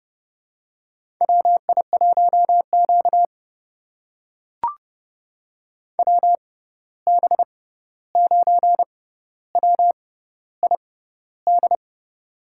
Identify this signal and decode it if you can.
Morse code, heard as a keyed tone, about 32 wpm, spaced as for 14 wpm, 700 Hz. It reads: WI1Q WB9WID
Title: USA Call Signs - 6x Character Spacing - 30wpm